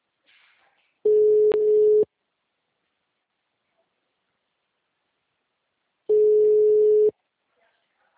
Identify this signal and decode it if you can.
Ringback tone (Europe, 425 Hz).
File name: out- -2053- - - .2194